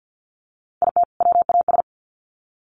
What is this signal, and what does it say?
Morse code, heard as a keyed tone, about 50 wpm, 700 Hz. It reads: IT WAS
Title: Sets of 2 Words 50wpm